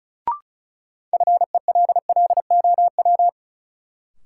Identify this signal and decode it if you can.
Morse code, heard as a keyed tone, about 36 wpm, 700 Hz. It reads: FELLOW
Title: Sets of 5 Words Encore - New Word Drill - 35wpm